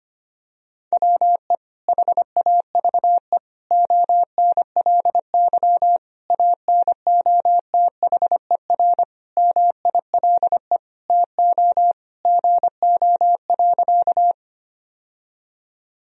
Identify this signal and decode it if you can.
Morse code, heard as a keyed tone, about 25 wpm, 700 Hz. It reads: WE HAVE ONLY ANOTHER MILE TO GO.
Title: Sentences from Top 1000 Words - Review Part 1 - 25wpm